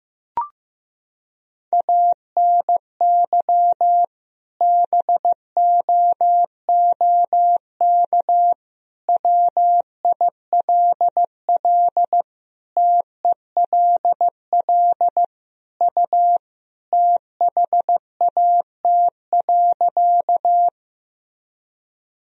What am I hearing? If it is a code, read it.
Morse code, heard as a keyed tone, about 16 wpm, 700 Hz. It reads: ANY BOOK WILL TELL U THAT.
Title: Sentences from Top 300 Words and Abbreviations 15wpm